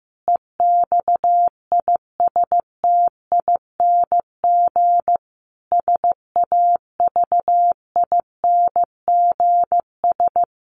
Morse code, heard as a keyed tone, about 15 wpm, 700 Hz. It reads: EXISTING SAVINGS
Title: Sentences from Top 1500 Words 15wpm